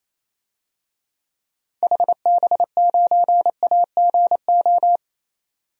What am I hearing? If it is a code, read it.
Morse code, heard as a keyed tone, about 28 wpm, 700 Hz. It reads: HB9AGO
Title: DXCC-100 Call Signs - 6 Characters - 28wpm